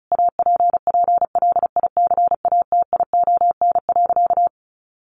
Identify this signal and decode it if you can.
Morse code, heard as a keyed tone, about 34 wpm, 700 Hz. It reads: APPLICATION.
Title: Sentences from Top 1200 Words 35wpm